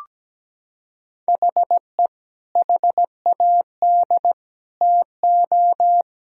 Morse code, heard as a keyed tone, about 17 wpm, 700 Hz. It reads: HE HAD TO
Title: Sentences from Top 1000 Words - Review Part 4 - 17wpm